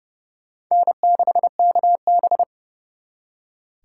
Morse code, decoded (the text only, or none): N6KB